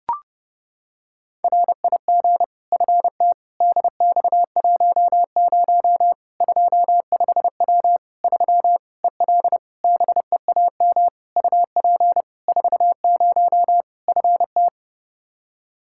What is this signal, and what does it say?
Morse code, heard as a keyed tone, about 32 wpm, 700 Hz. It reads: RIG FT DX10 25W 3 EL BEAM UP 40 FT
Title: Practice QSO - Exchange 2 - Info - 30wpm